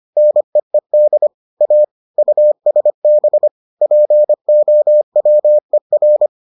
Morse code, heard as a keyed tone, about 25 wpm, 600 Hz. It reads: NEED A USB POWER